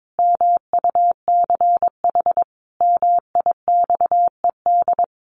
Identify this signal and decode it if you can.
Morse code, heard as a keyed tone, about 22 wpm, 700 Hz. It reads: MUCH MIXED
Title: Sentences from Top 1500 Words 22wpm